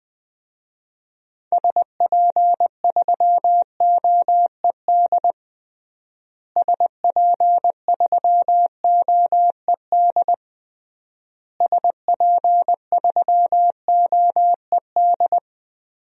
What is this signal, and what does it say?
Morse code, heard as a keyed tone, about 20 wpm, 700 Hz. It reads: SP3OED SP3OED SP3OED
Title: DXCC-100 Call Signs - 6 Characters - Repeated 3x - 20wpm